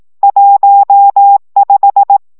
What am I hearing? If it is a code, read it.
Morse code, heard as a keyed tone, about 18 wpm, 800 Hz. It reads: 15
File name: szamolo / CW 15 baud